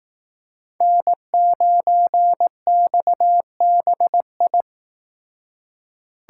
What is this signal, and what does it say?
Morse code, heard as a keyed tone, about 19 wpm, 700 Hz. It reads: N9XBI
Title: Speed Racing - USA 1x3 Call Signs 15wpm